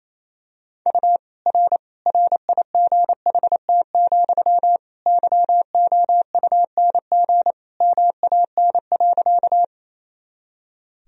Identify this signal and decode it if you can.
Morse code, heard as a keyed tone, about 27 wpm, 700 Hz. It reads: U R RIGHT, YOUNG MAN.
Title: Sentences from Top 400 Words and Abbreviations 28wpm